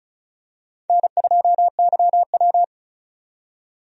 Morse code, heard as a keyed tone, about 36 wpm, 690 Hz. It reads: N2YW
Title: USA Call Signs - 4 Characters - 35wpm